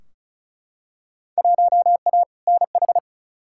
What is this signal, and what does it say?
Morse code, heard as a keyed tone, about 36 wpm, 690 Hz. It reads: 1A NH
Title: Field Day Exchanges Encore - 35wpm